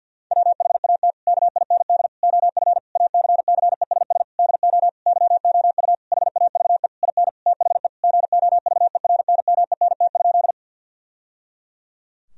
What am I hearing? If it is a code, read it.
Morse code, heard as a keyed tone, about 50 wpm, 700 Hz. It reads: WHAT KIND OF ACCESS DO YOU HAVE IN THE GOVERNMENT?
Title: Sentences from Top 1000 Words - Review Part 4 - 50wpm